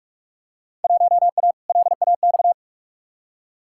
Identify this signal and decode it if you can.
Morse code, heard as a keyed tone, about 46 wpm, 700 Hz. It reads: 1A LAX
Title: Field Day Exchanges - 45wpm